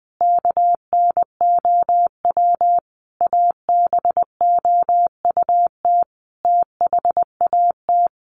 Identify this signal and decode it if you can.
Morse code, heard as a keyed tone, about 20 wpm, 700 Hz. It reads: KNOW ABOUT THAT
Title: Sets of 5 Words Encore2 20wpm